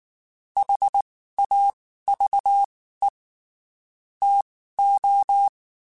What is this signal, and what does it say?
Morse code, heard as a keyed tone, about 19 wpm, spaced as for 14 wpm, 800 Hz. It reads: HAVE TO
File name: (spaced) / QSOs Spaced-mp3s / kc7sac @18 sp